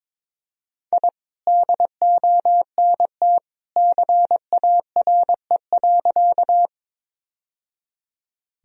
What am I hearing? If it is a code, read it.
Morse code, heard as a keyed tone, about 22 wpm, 700 Hz. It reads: I DONT CARE.
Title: Sentences from Top 1000 Words - Review Part 1 - 22wpm